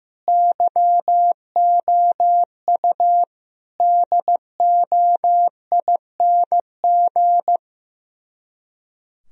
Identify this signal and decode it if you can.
Morse code, heard as a keyed tone, about 15 wpm, 700 Hz. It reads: YOU DOING
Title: Speed-Racing - Sets of 4 Words 15wpm